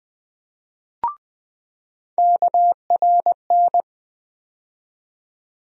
Morse code, heard as a keyed tone, about 20 wpm, 700 Hz. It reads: KRN
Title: USA Call Sign Postfixes - 3 Letters 20wpm